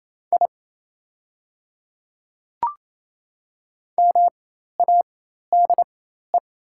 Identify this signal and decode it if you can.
Morse code, heard as a keyed tone, about 27 wpm, spaced as for 13 wpm, 690 Hz. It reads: I MADE
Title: 4x Character Spacing - Top 200 Words - 28wpm